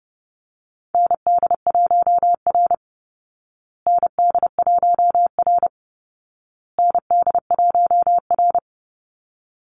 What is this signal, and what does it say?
Morse code, heard as a keyed tone, about 30 wpm, 700 Hz. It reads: ND1R ND1R ND1R
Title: Call Signs Repeated 2x - 30wpm